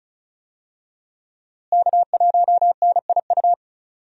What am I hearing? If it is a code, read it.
Morse code, heard as a keyed tone, about 36 wpm, 700 Hz. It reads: K1NIU
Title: USA Call Signs - 5 Characters - Repeated 2x - 35wpm